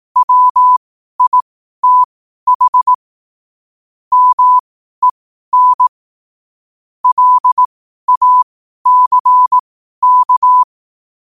Morse code, heard as a keyed tone, about 18 wpm, spaced as for 12 wpm, 1000 Hz. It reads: WITH MEN LACK
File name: qotdfiles / 12wpm